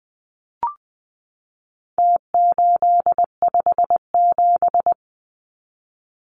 Morse code, heard as a keyed tone, about 20 wpm, 700 Hz. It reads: T857
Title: Numbers - Contest Style - 4-Digit 20wpm